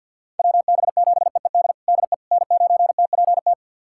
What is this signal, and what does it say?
Morse code, heard as a keyed tone, about 46 wpm, 700 Hz. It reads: WB7EED DE N9TPT
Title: USA Callsign DE Callsign 50wpm